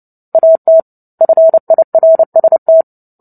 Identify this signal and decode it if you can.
Morse code, heard as a keyed tone, about 30 wpm, 650 Hz. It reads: AT FIRST
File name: Book Two / 03-Chapter 3 - 29 WPM